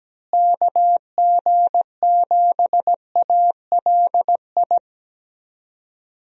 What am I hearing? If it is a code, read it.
Morse code, heard as a keyed tone, about 17 wpm, 700 Hz. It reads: KG7ALI
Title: USA Call Signs - 6 Characters - Repeated 2x - 17wpm